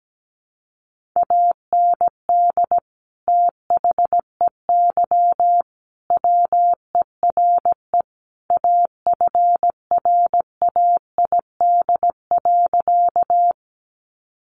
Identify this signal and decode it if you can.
Morse code, heard as a keyed tone, about 17 wpm, 700 Hz. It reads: AND THEY WERE AFRAID.
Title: Sentences from Top 1500 Words 17wpm